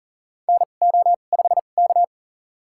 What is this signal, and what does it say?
Morse code, heard as a keyed tone, about 38 wpm, spaced as for 28 wpm, 700 Hz. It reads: NO5K
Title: USA Call Signs - 4 Characters - 2x Character Spacing - 40wpm